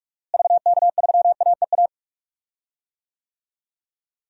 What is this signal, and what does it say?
Morse code, heard as a keyed tone, about 46 wpm, 700 Hz. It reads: VK3AEA
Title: DXCC-100 Call Signs - 6 Characters - Repeated 2x - 45wpm